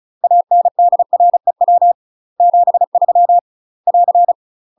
Morse code, heard as a keyed tone, about 34 wpm, 700 Hz. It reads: ANDREW 73 <AR>
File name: UK MASTER QSOs TRAINING / 35WPM / UK MASTER QSOs 0004